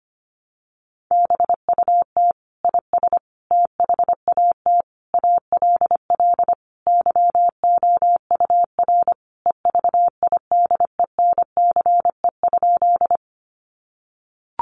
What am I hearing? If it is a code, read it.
Morse code, heard as a keyed tone, about 25 wpm, 700 Hz. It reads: BUT IS THAT ALL YOUR EVIDENCE?
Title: Sentences from Top 1300 Words 25wpm